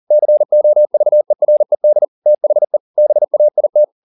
Morse code, heard as a keyed tone, about 40 wpm, 600 Hz. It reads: COVERED THE BAIT